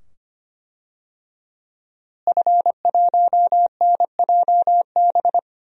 Morse code, heard as a keyed tone, about 25 wpm, 700 Hz. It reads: F1NJB
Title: DXCC-100 Call Signs - 5 Characters - Repeated 2x - 25wpm